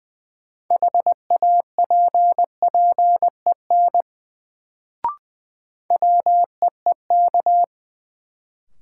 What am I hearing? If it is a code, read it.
Morse code, heard as a keyed tone, about 21 wpm, 700 Hz. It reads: HAPPEN WEEK
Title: Sets of 5 Words Encore - New Word Drill - 20wpm